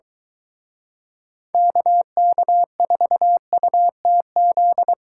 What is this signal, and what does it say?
Morse code, heard as a keyed tone, about 22 wpm, 700 Hz. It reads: KK4UTZ